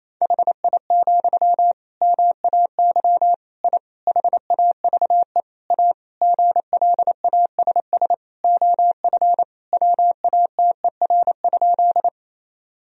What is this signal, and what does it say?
Morse code, heard as a keyed tone, about 27 wpm, 700 Hz. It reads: HI, MAY I HAVE A GLASS OF WATER?
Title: Sentences from Top 1000 Words - Review Part 3 - 28wpm